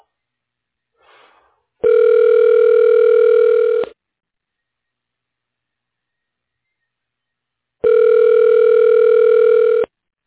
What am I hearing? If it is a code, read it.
Ringback tone (North America), 440 + 480 Hz.